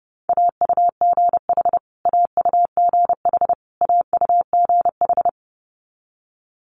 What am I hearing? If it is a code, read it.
Morse code, heard as a keyed tone, about 30 wpm, 700 Hz. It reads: AUGH AUGH AUGH
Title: Multi-letter Phonemes Repeated 3x 30wpm